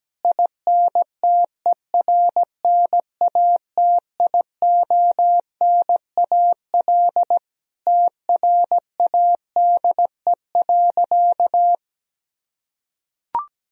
Morse code, heard as a keyed tone, about 17 wpm, 700 Hz. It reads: INTERNATIONAL TRADE.
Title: Sentences from Top 1000 Words - Review Part 4 - 17wpm